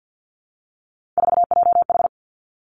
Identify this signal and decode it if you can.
Morse code, heard as a keyed tone, about 50 wpm, 700 Hz. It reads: 4JH
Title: Three Challenging Letter-Number Combinations 50wpm